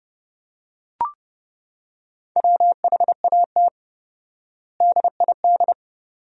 Morse code, heard as a keyed tone, about 30 wpm, 700 Hz. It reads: WHAT DID
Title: 4x Word-Spacing - Sentences from Top 200 Words - 30wpm